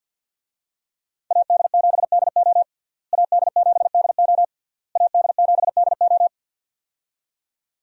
Morse code, heard as a keyed tone, about 50 wpm, 700 Hz. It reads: AD7DO AD7DO AD7DO
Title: USA Call Signs - 5 Characters - Repeated 3x - 50wpm